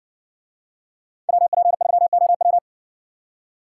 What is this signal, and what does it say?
Morse code, heard as a keyed tone, about 60 wpm, 700 Hz. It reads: WP3OW